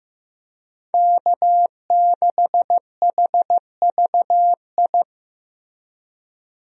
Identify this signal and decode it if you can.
Morse code, heard as a keyed tone, about 15 wpm, 700 Hz. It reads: K6HVI
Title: USA Call Signs - 5 Characters - 15wpm